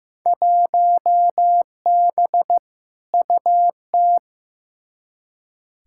Morse code, heard as a keyed tone, about 15 wpm, 700 Hz. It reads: 1B UT
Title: Field Day Exchanges - 15wpm